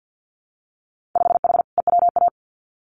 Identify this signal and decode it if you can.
Morse code, heard as a keyed tone, about 50 wpm, 700 Hz. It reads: 5H EWA
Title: Winter Field Day Exchanges Encore 50wpm